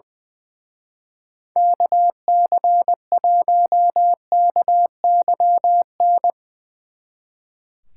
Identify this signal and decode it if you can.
Morse code, heard as a keyed tone, about 22 wpm, 700 Hz. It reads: KC1KYN
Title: Speed Racing - USA 2x3 Call Signs 20wpm